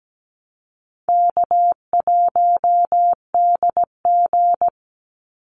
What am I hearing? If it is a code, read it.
Morse code, heard as a keyed tone, about 17 wpm, 700 Hz. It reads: K1DG
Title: Easy Call Signs Repeated 2x - 17wpm